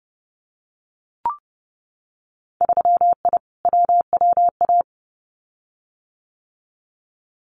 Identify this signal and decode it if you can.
Morse code, heard as a keyed tone, about 30 wpm, 700 Hz. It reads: 3I WWA
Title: Winter Field Day Exchanges Encore 30wpm